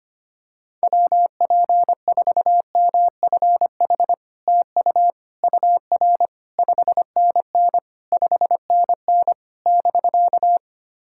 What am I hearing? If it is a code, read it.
Morse code, heard as a keyed tone, about 25 wpm, 700 Hz. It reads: WP4MFH TU UR 5NN 5NN <BK>
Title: SOTA QSO Practice 25wpm